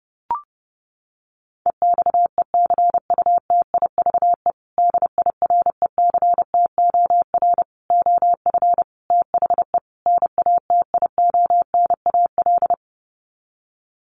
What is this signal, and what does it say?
Morse code, heard as a keyed tone, about 30 wpm, 700 Hz. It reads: EXECUTIVE DIRECTOR OF THE NATIONAL
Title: Sets of 5 Words Encore2 30wpm